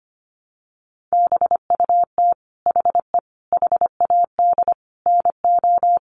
Morse code, heard as a keyed tone, about 25 wpm, 700 Hz. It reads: BUT HE HAD NO
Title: Sentences from Top 100 Words 25wpm